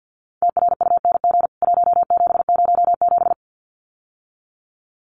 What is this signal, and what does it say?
Morse code, heard as a keyed tone, about 50 wpm, 700 Hz. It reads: TRUNG 1707
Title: CWT Exchange 50wpm